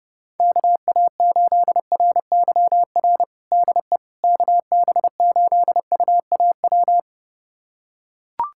Morse code, heard as a keyed tone, about 30 wpm, 700 Hz. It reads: KA8RYR DE KB8UAW
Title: USA Callsign DE Callsign 30wpm